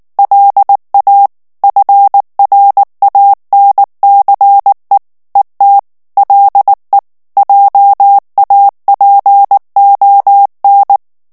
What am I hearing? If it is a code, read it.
Morse code, heard as a keyed tone, about 19 wpm, 800 Hz. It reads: LA FRANCE ET LE JAPON
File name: FRANCAIS TEXTE CLAIR 1 VITESSE 18 MOTS MINUTES